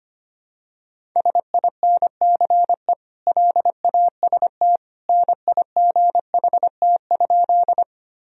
Morse code, heard as a keyed tone, about 25 wpm, 700 Hz. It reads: SINCE LAST NIGHT?
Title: Sentences from Top 1000 Words - Review Part 1 - 25wpm